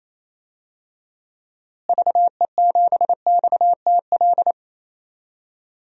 Morse code, heard as a keyed tone, about 28 wpm, 700 Hz. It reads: VE7XTL
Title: DXCC-100 Call Signs - 6 Characters - Repeated 3x - 28wpm